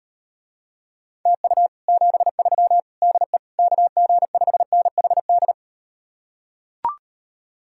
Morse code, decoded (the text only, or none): TU 73 DE KG5NHD